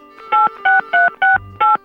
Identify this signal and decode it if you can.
DTMF, heard as telephone touch tones, tones of about 170 ms. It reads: 86368